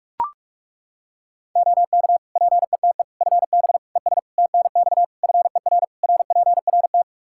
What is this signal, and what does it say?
Morse code, heard as a keyed tone, about 41 wpm, 700 Hz. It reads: OK PETE FB ES TNX FER RPRT
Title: Practice QSO - Exchange 2 - Info - 45wpm